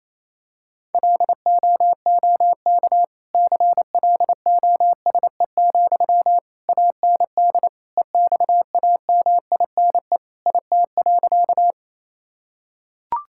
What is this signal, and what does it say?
Morse code, heard as a keyed tone, about 28 wpm, 700 Hz. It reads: LOOK CLOSE, AND EXAMINE IT.
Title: Sentences from Top 1600 Words 28wpm